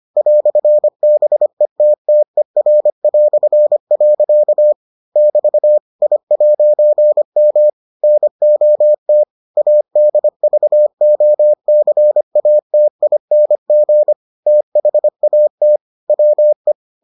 Morse code, heard as a keyed tone, about 25 wpm, 600 Hz. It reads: "BETTER". <BT> I'M NOT ADVOCATING THAT WE